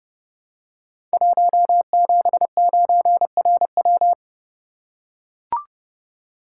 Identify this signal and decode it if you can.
Morse code, heard as a keyed tone, about 29 wpm, 700 Hz. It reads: 179RW
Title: Code Groups - 5 Letter:Numbers - 30wpm